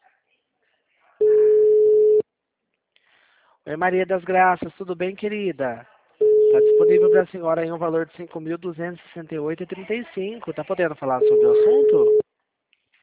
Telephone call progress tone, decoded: ringback tone

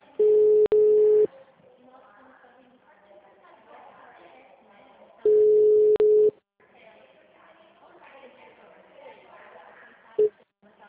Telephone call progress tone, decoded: ringback tone

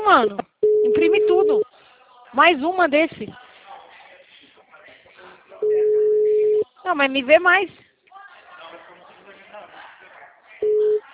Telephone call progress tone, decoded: ringback tone

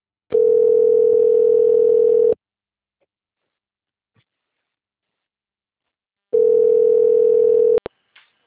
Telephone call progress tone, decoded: ringback tone

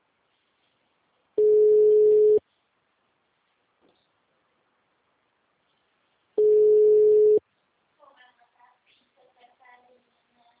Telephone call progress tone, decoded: ringback tone

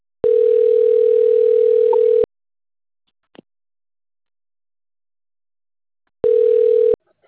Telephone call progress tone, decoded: ringback tone

